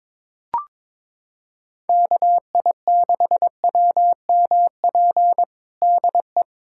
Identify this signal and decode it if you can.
Morse code, heard as a keyed tone, about 21 wpm, 700 Hz. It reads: KI6WMP DE
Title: USA Callsign DE Callsign 22wpm